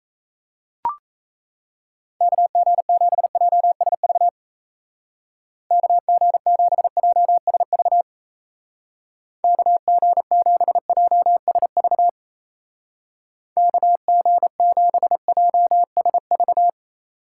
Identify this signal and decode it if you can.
Morse code, heard as a keyed tone, about 34 wpm, 700 Hz. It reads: KG7JSV KG7JSV KG7JSV KG7JSV
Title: Speed Racing - USA 2x3 Call Signs 28wpm